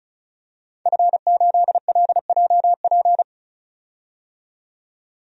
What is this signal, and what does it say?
Morse code, heard as a keyed tone, about 36 wpm, 700 Hz. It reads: F8LJP
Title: DXCC-100 Call Signs - 5 Characters - 35wpm